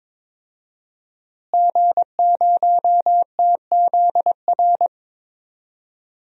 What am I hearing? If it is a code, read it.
Morse code, heard as a keyed tone, about 22 wpm, 700 Hz. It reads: G0TZR
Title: DXCC-100 Call Signs - 5 Characters - Repeated 2x - 22wpm